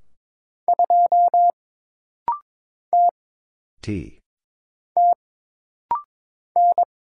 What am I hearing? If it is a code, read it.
Morse code, heard as a keyed tone, about 22 wpm, 700 Hz. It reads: 2 T T N
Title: Single Letter-Number Alt - Rapid-Fire 22wpm